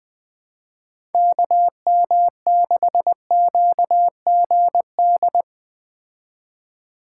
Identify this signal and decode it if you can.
Morse code, heard as a keyed tone, about 20 wpm, 700 Hz. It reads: KM6QGD
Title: Speed Racing - USA 2x3 Call Signs 17wpm